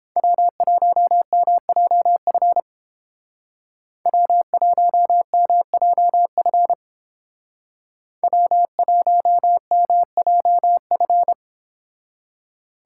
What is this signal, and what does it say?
Morse code, heard as a keyed tone, about 29 wpm, 700 Hz. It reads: W1MJF W1MJF W1MJF